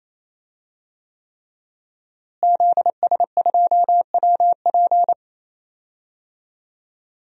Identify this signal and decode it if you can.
Morse code, heard as a keyed tone, about 28 wpm, 700 Hz. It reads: ZS2WP
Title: DXCC-100 Call Signs - 5 Characters - Repeated 3x - 28wpm